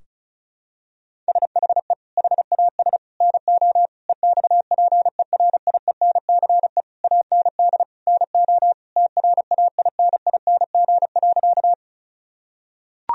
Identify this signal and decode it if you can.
Morse code, heard as a keyed tone, about 34 wpm, 700 Hz. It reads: SHE HAS NO EXPERIENCE AND NO TRAINING.
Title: Sentences from Top 1000 Words - Review Part 4 - 35wpm